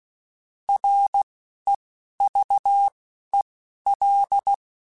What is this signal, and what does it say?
Morse code, heard as a keyed tone, about 16 wpm, 800 Hz. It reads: R E V E L